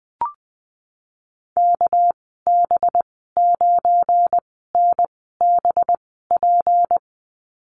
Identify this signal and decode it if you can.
Morse code, heard as a keyed tone, about 19 wpm, spaced as for 14 wpm, 700 Hz. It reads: KB9NBP
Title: 207 Lesson 33g - Letter j - Call Signs 20wpm